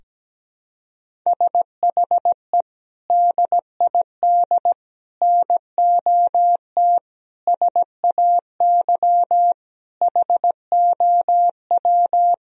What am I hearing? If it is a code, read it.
Morse code, heard as a keyed tone, about 17 wpm, 700 Hz. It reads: SHE DID NOT SAY HOW
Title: Sentences from Top 1000 Words - Review Part 1 - 17wpm